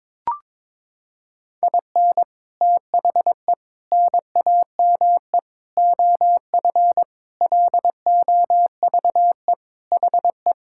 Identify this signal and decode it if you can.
Morse code, heard as a keyed tone, about 22 wpm, 700 Hz. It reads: IN THE NAME OF LOVE HE
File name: practice / 22wpm / Sentences with 4-Word Phrases - 22wpm